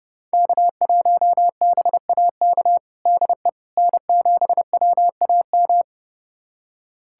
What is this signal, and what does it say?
Morse code, heard as a keyed tone, about 30 wpm, 700 Hz. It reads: K1BAK DE N7WAM
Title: USA Callsign DE Callsign - 5 Characters - 30wpm